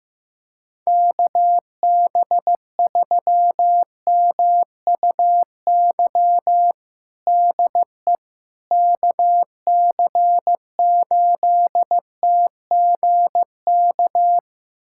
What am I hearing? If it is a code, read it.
Morse code, heard as a keyed tone, about 15 wpm, 700 Hz. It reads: KB3MUY DE KC8TGK
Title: USA Callsign DE Callsign - 6 Characters - 15wpm